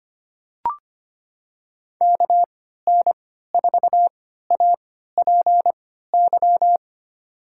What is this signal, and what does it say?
Morse code, heard as a keyed tone, about 25 wpm, spaced as for 14 wpm, 700 Hz. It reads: KN4APY
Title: USA Call Signs - 6 Characters - 3x Character Spacing - 25wpm